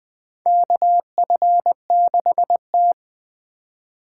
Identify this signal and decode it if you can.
Morse code, heard as a keyed tone, about 21 wpm, 700 Hz. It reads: KF6T
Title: K1USN SST QSO Practice - 20wpm